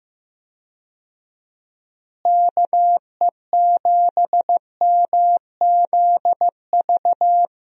Morse code, heard as a keyed tone, about 15 wpm, 700 Hz. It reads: KE7MZV
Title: USA Call Signs - 6 Characters - Repeated 2x - 15wpm